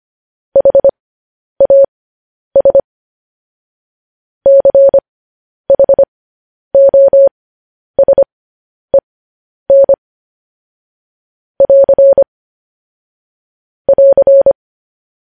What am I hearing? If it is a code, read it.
Morse code, heard as a keyed tone, about 25 wpm, spaced as for 10 wpm, 550 Hz. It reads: HAS CHOSEN <AR> <AR>
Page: news-headlines-in-morse-code-at-10-wpm